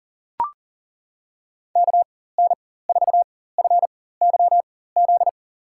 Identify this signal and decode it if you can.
Morse code, heard as a keyed tone, about 38 wpm, spaced as for 18 wpm, 700 Hz. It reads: KN4FYZ
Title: USA Call Signs - 6 Characters - 4x Character Spacing - 40wpm